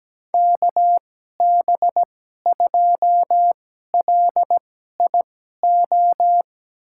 Morse code, heard as a keyed tone, about 17 wpm, spaced as for 12 wpm, 700 Hz. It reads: KB2LIO
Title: USA Call Signs - 6 Characters - 2x Character Spacing - 17wpm